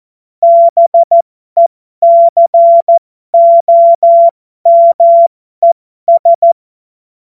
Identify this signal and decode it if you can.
Morse code, heard as a keyed tone, about 14 wpm, spaced as for 12 wpm, 680 Hz. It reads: BECOMES
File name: Morsekurs / Witze